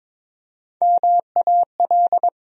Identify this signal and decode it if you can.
Morse code, heard as a keyed tone, about 22 wpm, 700 Hz. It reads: MAL
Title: USA Call Sign Postfixes - 3 Letters 22wpm